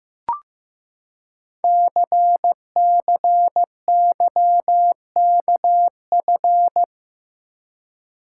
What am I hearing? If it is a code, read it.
Morse code, heard as a keyed tone, about 15 wpm, 700 Hz. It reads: CCYKF
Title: Code Groups - 5 Letters - 15wpm